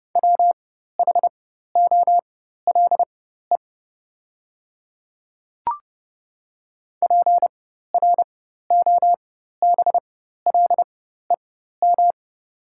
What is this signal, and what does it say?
Morse code, heard as a keyed tone, about 30 wpm, spaced as for 14 wpm, 690 Hz. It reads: WHOLE PROBLEM
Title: 4x Character Spacing - Top 500 Words - 30wpm